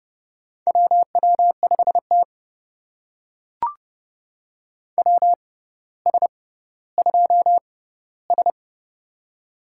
Morse code, heard as a keyed tone, about 30 wpm, spaced as for 15 wpm, 700 Hz. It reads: WW5T WS2S
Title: USA Call Signs - 4 Characters - 6x Character Spacing - 30wpm